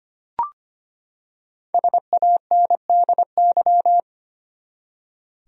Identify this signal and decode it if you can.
Morse code, heard as a keyed tone, about 25 wpm, 700 Hz. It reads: SANDY
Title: USA Common Names - Max 5 Letters - 25wpm